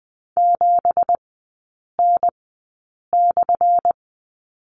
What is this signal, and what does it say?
Morse code, heard as a keyed tone, about 20 wpm, spaced as for 8 wpm, 690 Hz. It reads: 7N/
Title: 198 Lesson 32e - Letter k - Three Characters 20wpm